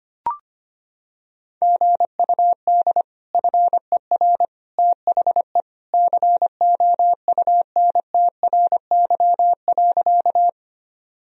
Morse code, heard as a keyed tone, about 25 wpm, 700 Hz. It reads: GUD FER THE COUNTRY.
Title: Sentences from Top 300 Words and Abbreviations 25wpm